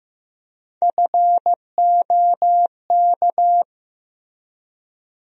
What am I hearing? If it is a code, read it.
Morse code, heard as a keyed tone, about 15 wpm, 700 Hz. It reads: FOK
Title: USA Call Sign Postfixes - 3 Letters 15wpm